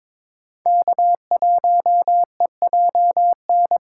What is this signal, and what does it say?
Morse code, heard as a keyed tone, about 22 wpm, 700 Hz. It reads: K1EJN
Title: USA Call Signs - 5 Characters - 3x Character Spacing - 22wpm